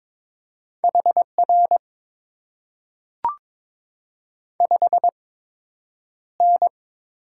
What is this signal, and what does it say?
Morse code, heard as a keyed tone, about 22 wpm, 700 Hz. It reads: H R 5 N
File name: Top 90 QSO Elements - 8x Character Spacing - 22wpm